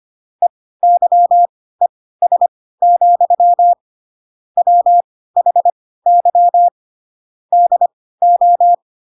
Morse code, heard as a keyed tone, about 25 wpm, spaced as for 16 wpm, 700 Hz. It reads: EYES, WHY DO
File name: Pinocchio / Pinocchio ch3 10 25 ik1hge